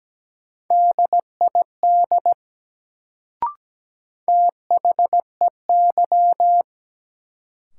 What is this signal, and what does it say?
Morse code, heard as a keyed tone, about 17 wpm, 700 Hz. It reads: DID THEY
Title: Sentences from Top 100 Words - New Word Drill - 17wpm